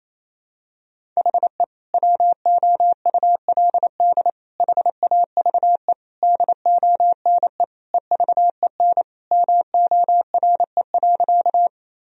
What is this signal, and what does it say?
Morse code, heard as a keyed tone, about 28 wpm, 700 Hz. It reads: HE WOULD HAVE DONE EVEN MORE.
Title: Sentences from Top 1000 Words - Review Part 1 - 28wpm